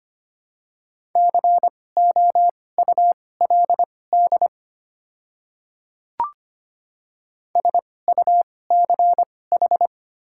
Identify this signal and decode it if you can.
Morse code, heard as a keyed tone, about 25 wpm, spaced as for 18 wpm, 700 Hz. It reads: COULD SUCH
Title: 2x Character Spacing - Top 200 Words - 25wpm